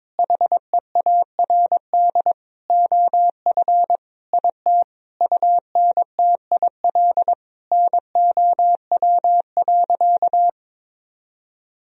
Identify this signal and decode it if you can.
Morse code, heard as a keyed tone, about 22 wpm, 700 Hz. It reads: HEARD OF IT UNTIL NOW.
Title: Sentences from Top 1000 Words - Review Part 1 - 22wpm